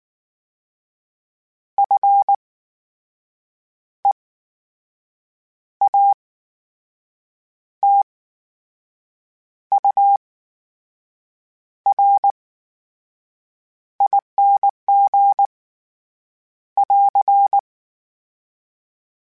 Morse code, heard as a keyed tone, about 19 wpm, spaced as for 6 wpm, 800 Hz. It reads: FEATURING <AR>
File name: Special INGs / Special INGs-mp3s / VerySlowToFaster Ing @18